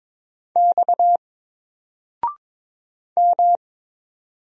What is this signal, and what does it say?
Morse code, heard as a keyed tone, about 22 wpm, 700 Hz. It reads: X M
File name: Single-Letter-Number-22wpm